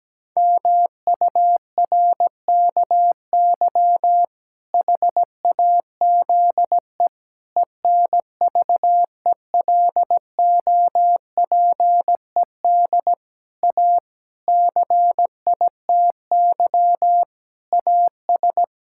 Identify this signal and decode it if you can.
Morse code, heard as a keyed tone, about 17 wpm, 700 Hz. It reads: MURKY HAZE ENVELOPED A CITY AS
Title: Pangram Challenge v2 - 17wpm